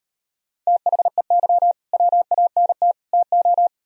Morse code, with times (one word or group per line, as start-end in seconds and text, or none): 0.7-1.7 THEY
1.9-2.9 WANT
3.1-3.7 TO